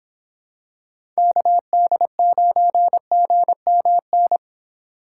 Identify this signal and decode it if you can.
Morse code, heard as a keyed tone, about 25 wpm, 700 Hz. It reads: KD9GMN